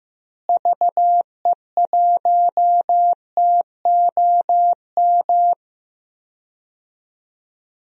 Morse code, heard as a keyed tone, about 15 wpm, 700 Hz. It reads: VE1TOM
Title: DXCC-100 Call Signs - 6 Characters - Repeated 3x - 15wpm